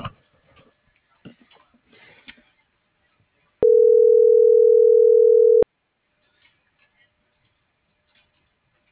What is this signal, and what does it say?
Ringback tone (North America), 440 + 480 Hz.